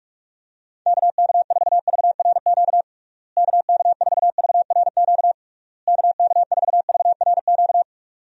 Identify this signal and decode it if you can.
Morse code, heard as a keyed tone, about 45 wpm, 700 Hz. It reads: KK4VRQ KK4VRQ KK4VRQ
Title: USA Call Signs - 6 Characters - Repeated 3x - 45wpm